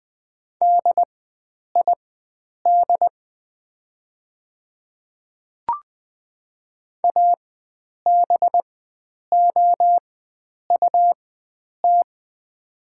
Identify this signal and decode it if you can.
Morse code, heard as a keyed tone, about 20 wpm, spaced as for 9 wpm, 700 Hz. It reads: DID ABOUT
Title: 4x Character Spacing - Top 100 Words - 20wpm